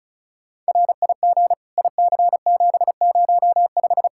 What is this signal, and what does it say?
Morse code, heard as a keyed tone, about 34 wpm, 700 Hz. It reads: RIG IC705
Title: Practice QSO - Exchange 2 - Info - 35wpm